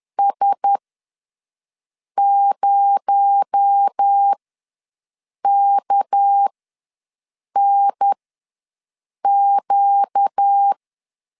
Morse code, heard as a keyed tone, about 10 wpm, spaced as for 5 wpm, 800 Hz. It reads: S0KNQ
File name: LEZIONI / 6 PKX0 / 6 PKX0 ws 10wpm 4220